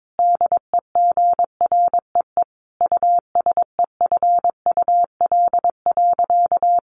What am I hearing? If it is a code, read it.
Morse code, heard as a keyed tone, about 22 wpm, 700 Hz. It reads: DEGREE USEFUL.
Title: Sentences from Top 1500 Words 22wpm